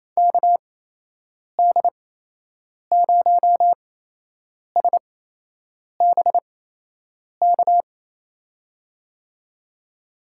Morse code, heard as a keyed tone, about 28 wpm, spaced as for 8 wpm, 700 Hz. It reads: KD0SBK
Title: USA Call Signs - 6 Characters - 8x Character Spacing - 28wpm